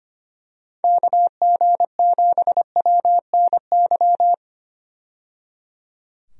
Morse code, heard as a keyed tone, about 25 wpm, 700 Hz. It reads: KG7WNY
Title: USA Call Signs - 6 Characters - 25wpm